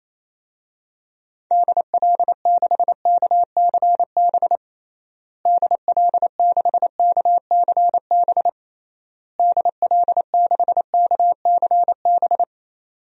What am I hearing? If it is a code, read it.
Morse code, heard as a keyed tone, about 28 wpm, 700 Hz. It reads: DL6KCB DL6KCB DL6KCB
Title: DXCC-100 Call Signs - 6 Characters - Repeated 3x - 28wpm